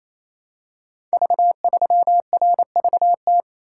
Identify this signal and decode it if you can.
Morse code, heard as a keyed tone, about 27 wpm, 700 Hz. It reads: V3RVT
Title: Code Groups - 5 Letter:Numbers - 28wpm